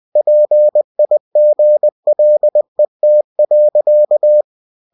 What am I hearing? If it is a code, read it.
Morse code, heard as a keyed tone, about 20 wpm, 600 Hz. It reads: PIGLET.
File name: cw-books / winnie-the-pooh / winnie-20wpm-page-0099